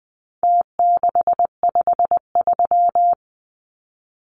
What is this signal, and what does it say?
Morse code, heard as a keyed tone, about 20 wpm, 700 Hz. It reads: T653
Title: Numbers - Contest Style - 4-Digit 20wpm